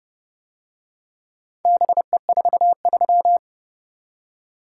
Morse code, heard as a keyed tone, about 29 wpm, 700 Hz. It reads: BE43